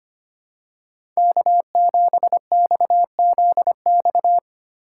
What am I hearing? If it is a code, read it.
Morse code, heard as a keyed tone, about 25 wpm, 700 Hz. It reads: K7XZX
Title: USA Call Signs - 5 Characters - 4x Character Spacing - 25wpm